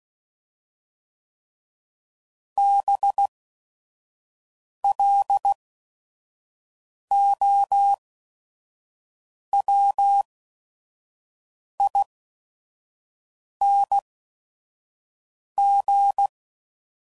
Morse code, heard as a keyed tone, about 16 wpm, spaced as for 5 wpm, 800 Hz. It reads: BLOWING